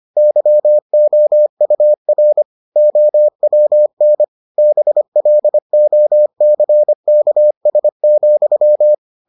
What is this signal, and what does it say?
Morse code, heard as a keyed tone, about 25 wpm, 600 Hz. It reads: YOUR OWN BLOCKS,